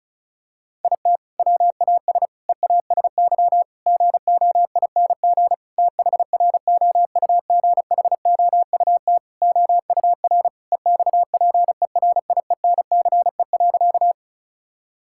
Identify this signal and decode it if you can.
Morse code, heard as a keyed tone, about 34 wpm, 700 Hz. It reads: IT WAS EASY GOING THROUGHOUT OUR EXPERIENCE.
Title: Sentences from Top 1000 Words - Review Part 5 - 35wpm